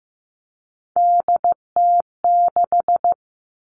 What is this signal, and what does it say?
Morse code, heard as a keyed tone, about 15 wpm, 700 Hz. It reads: DT6
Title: All Cut Numbers - 3-Digit 15wpm